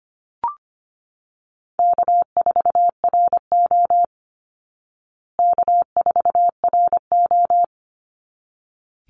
Morse code, heard as a keyed tone, about 25 wpm, 700 Hz. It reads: K4RO K4RO
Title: Easy Call Signs Repeated 2x - 25wpm